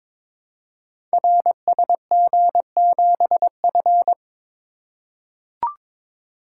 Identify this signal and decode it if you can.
Morse code, heard as a keyed tone, about 22 wpm, 700 Hz. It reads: RSG7F
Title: Code Groups - 5 Letter:Numbers - 22wpm